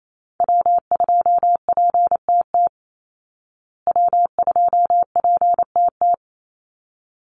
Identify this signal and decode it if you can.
Morse code, heard as a keyed tone, about 28 wpm, 700 Hz. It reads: W2PTT W2PTT
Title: Call Signs Repeated 3x - 28wpm